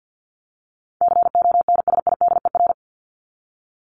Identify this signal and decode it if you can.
Morse code, heard as a keyed tone, about 50 wpm, 700 Hz. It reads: CONSIDER
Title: Top 501-750 Words 50wpm